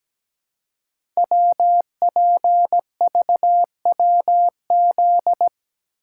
Morse code, heard as a keyed tone, about 17 wpm, 700 Hz. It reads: WPVWZ
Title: Code Groups - 5 Letters - 17wpm